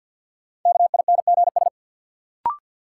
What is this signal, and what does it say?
Morse code, heard as a keyed tone, about 51 wpm, 700 Hz. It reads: KINGS
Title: Sentences from Top 1000 Words - Review Part 6 - New Word Drill - 50wpm